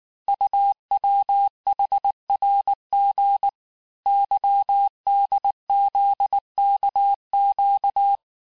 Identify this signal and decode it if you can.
Morse code, heard as a keyed tone, about 19 wpm, 800 Hz. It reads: UWHRG YDZKQ
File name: cours-du-lundi files / cours-1595-teste-CHIFFRE-vitesse18-MPM